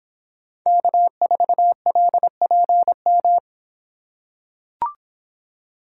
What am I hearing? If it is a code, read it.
Morse code, heard as a keyed tone, about 26 wpm, 700 Hz. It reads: K4LPM